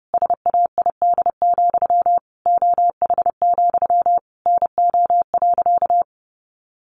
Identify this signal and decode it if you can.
Morse code, heard as a keyed tone, about 30 wpm, 700 Hz. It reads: SAID, OH, NO.
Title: Sentences from Top 1200 Words 30wpm